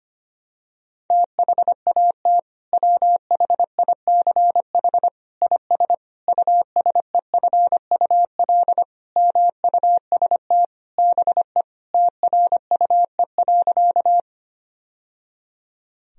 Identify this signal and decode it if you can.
Morse code, heard as a keyed tone, about 25 wpm, 700 Hz. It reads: THAT WHICH IS USEFUL MUST BE TRUE.
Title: Sentences from Top 1000 Words - Review Part 4 - 25wpm